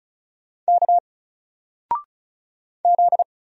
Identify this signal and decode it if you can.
Morse code, heard as a keyed tone, about 34 wpm, 700 Hz. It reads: K Z
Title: Single Letter-Number 35wpm